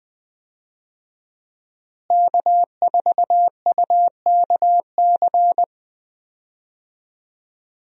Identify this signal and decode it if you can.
Morse code, heard as a keyed tone, about 20 wpm, 700 Hz. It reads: K4UKC
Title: USA Call Signs - 5 Characters - Repeated 3x - 20wpm